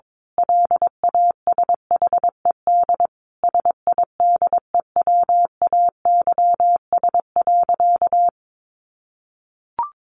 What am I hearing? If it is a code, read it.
Morse code, heard as a keyed tone, about 22 wpm, 700 Hz. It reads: LASHED SIDEWAYS.
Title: A Sound Of Thunder 22wpm